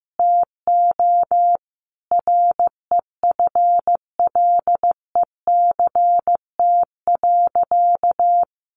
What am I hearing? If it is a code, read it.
Morse code, heard as a keyed tone, about 15 wpm, 700 Hz. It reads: TO REFLECT.
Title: Sentences from Top 1500 Words 15wpm